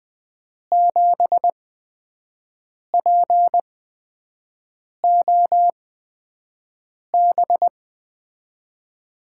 Morse code, heard as a keyed tone, about 20 wpm, spaced as for 5 wpm, 700 Hz. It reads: 7POB
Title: USA Call Signs - 5 Characters - 8x Character Spacing - 20wpm